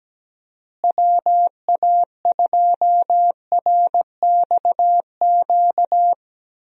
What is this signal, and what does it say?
Morse code, heard as a keyed tone, about 17 wpm, 700 Hz. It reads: WA2RXQ
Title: USA Call Signs - 6 Characters - Repeated 2x - 17wpm